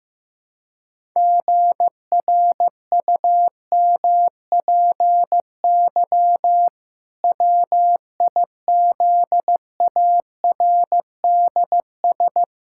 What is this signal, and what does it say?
Morse code, heard as a keyed tone, about 15 wpm, 700 Hz. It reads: GRUMPY WIZARDS
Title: Pangram Challenge v2 - 15wpm